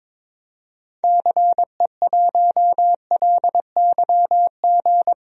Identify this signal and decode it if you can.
Morse code, heard as a keyed tone, about 22 wpm, 700 Hz. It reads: CE1LYG
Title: DXCC-100 Call Signs - 6 Characters - Repeated 3x - 22wpm